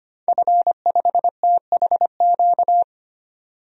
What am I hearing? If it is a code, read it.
Morse code, heard as a keyed tone, about 25 wpm, 700 Hz. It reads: F5THQ
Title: DXCC-100 Call Signs - 5 Characters - 25wpm